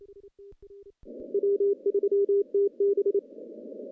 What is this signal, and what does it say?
Morse code, heard as a keyed tone, about 28 wpm, 400 Hz. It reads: W3TB
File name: contests / CW-SS-k5tr- -0200